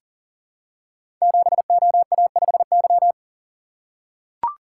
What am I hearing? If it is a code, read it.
Morse code, heard as a keyed tone, about 43 wpm, 700 Hz. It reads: 7OA5Y
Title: Code Groups - 5 Letter:Numbers - 40wpm